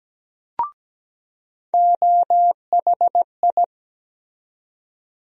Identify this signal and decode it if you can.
Morse code, heard as a keyed tone, about 17 wpm, 700 Hz. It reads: OHI
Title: USA Call Sign Postfixes - 3 Letters 17wpm